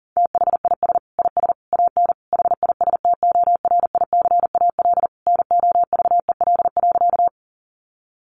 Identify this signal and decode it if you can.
Morse code, heard as a keyed tone, about 40 wpm, 700 Hz. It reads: THIS IS AN HISTORICAL NOVEL.
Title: Sentences from Top 1300 Words 40wpm